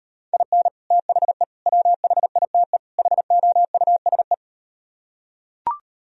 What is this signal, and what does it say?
Morse code, heard as a keyed tone, about 38 wpm, 700 Hz. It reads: IN THE WHITE HOUSE